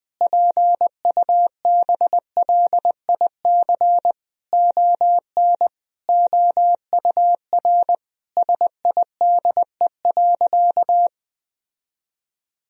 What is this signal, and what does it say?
Morse code, heard as a keyed tone, about 20 wpm, 700 Hz. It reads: PUBLIC ON OUR SIDE.
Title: Sentences from Top 1000 Words - Review Part 3 - 20wpm